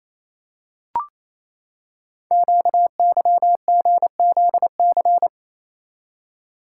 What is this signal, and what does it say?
Morse code, heard as a keyed tone, about 29 wpm, 700 Hz. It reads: QYGZC
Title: Code Groups - 5 Letters - 28wpm